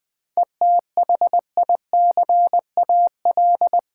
Morse code, heard as a keyed tone, about 20 wpm, 690 Hz. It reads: ETHICAL
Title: 7 Letter Words - First 3 Letters Given - 20wpm